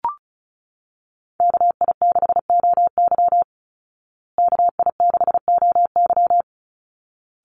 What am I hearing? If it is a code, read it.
Morse code, heard as a keyed tone, about 34 wpm, 700 Hz. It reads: KI6OY KI6OY
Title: Call Signs Repeated 3x - 35wpm